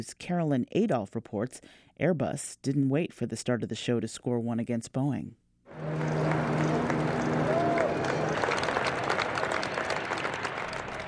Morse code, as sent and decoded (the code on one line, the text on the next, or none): none